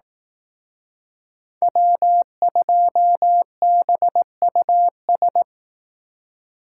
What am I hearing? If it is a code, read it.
Morse code, heard as a keyed tone, about 18 wpm, 700 Hz. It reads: W2BUS
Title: Speed Racing - USA 1x3 Call Signs 15wpm